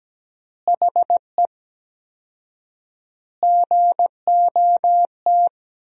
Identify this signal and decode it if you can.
Morse code, heard as a keyed tone, about 17 wpm, 700 Hz. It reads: HE GOT
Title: 4x Word-Spacing - Sentences from Top 300 Words - 17wpm